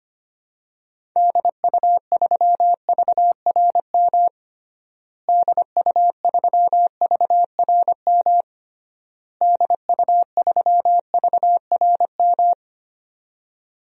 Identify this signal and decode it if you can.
Morse code, heard as a keyed tone, about 25 wpm, 700 Hz. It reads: DU3VRM DU3VRM DU3VRM
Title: DXCC-100 Call Signs - 6 Characters - Repeated 3x - 25wpm